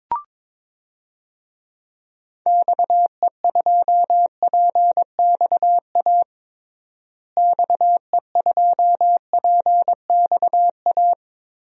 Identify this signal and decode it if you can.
Morse code, heard as a keyed tone, about 22 wpm, 700 Hz. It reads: XE2PXA XE2PXA
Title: DXCC-100 Call Signs - 6 Characters - Repeated 3x - 22wpm